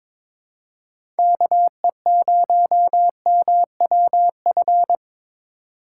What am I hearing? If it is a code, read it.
Morse code, heard as a keyed tone, about 22 wpm, 700 Hz. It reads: KE0MWF
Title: USA Call Signs - 3x Character Spacing - 22wpm